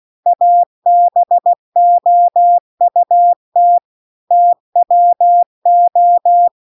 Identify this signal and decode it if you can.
Morse code, heard as a keyed tone, about 16 wpm, 700 Hz. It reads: ABOUT TWO
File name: 16WPM